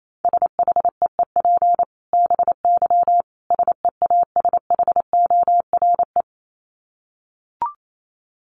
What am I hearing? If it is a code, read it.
Morse code, heard as a keyed tone, about 28 wpm, 700 Hz. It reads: SHEEP BY SEASHORE
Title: Dit Challenge 28wpm